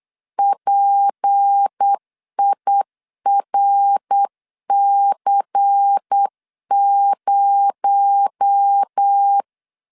Morse code, heard as a keyed tone, about 8 wpm, 800 Hz. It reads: PIRC0
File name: LEZIONI / 6 PKX0 / 6 PKX0 8wpm 3341